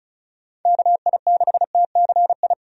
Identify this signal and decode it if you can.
Morse code, heard as a keyed tone, about 36 wpm, 700 Hz. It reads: KI6TCI
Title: USA Call Signs - 6 Characters - 35wpm